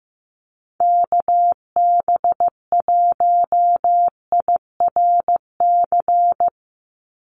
Morse code, heard as a keyed tone, about 15 wpm, 700 Hz. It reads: KB1IRC
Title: USA Call Signs 15wpm